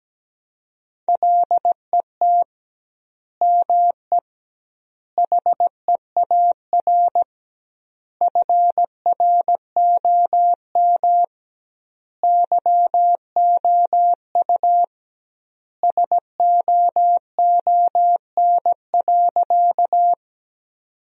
Morse code, heard as a keyed tone, about 17 wpm, 700 Hz. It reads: LET ME HEAR FROM YOU SOON.
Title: 2x Word-Spacing - Sentences from Top 400 Words - 17wpm